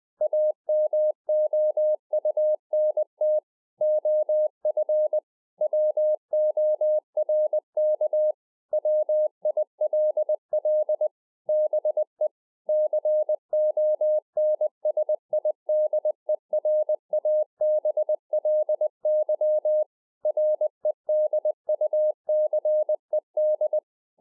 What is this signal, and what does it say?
Morse code, heard as a keyed tone, about 20 wpm, 610 Hz. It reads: AMOUNT OF WORK WILL BE CONSIDERABLY REDUCED
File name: 20WPM